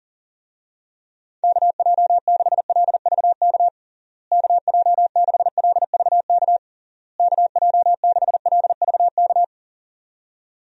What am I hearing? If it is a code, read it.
Morse code, heard as a keyed tone, about 40 wpm, 700 Hz. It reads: KJ6LVK KJ6LVK KJ6LVK
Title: USA Call Signs - 6 Characters - Repeated 3x - 40wpm